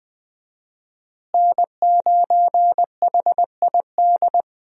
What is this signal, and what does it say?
Morse code, heard as a keyed tone, about 20 wpm, 700 Hz. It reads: N9HID
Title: USA Call Signs - 5 Characters - Repeated 2x - 20wpm